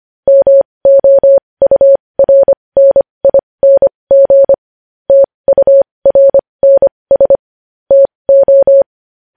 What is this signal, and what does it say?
Morse code, heard as a keyed tone, about 25 wpm, spaced as for 20 wpm, 550 Hz. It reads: MOURNING TURNS TO
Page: news headlines in morse code at 20 wpm